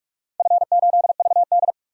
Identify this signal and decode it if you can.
Morse code, heard as a keyed tone, about 46 wpm, 700 Hz. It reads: F8VD
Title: DXCC-100 Call Signs - 4 Characters - 45wpm